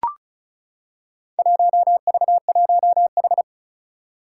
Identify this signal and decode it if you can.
Morse code, heard as a keyed tone, about 36 wpm, 700 Hz. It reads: 1V1H
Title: Four Challenging Letter-Number Combinations 35wpm